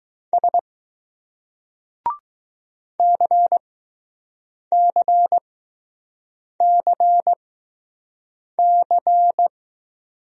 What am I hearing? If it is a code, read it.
Morse code, heard as a keyed tone, about 19 wpm, spaced as for 6 wpm, 700 Hz. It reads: S CCCC